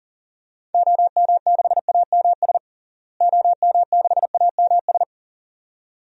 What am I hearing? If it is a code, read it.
Morse code, heard as a keyed tone, about 38 wpm, 700 Hz. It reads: OM6AMS OM6AMS
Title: DXCC-100 Call Signs - 6 Characters - Repeated 2x - 40wpm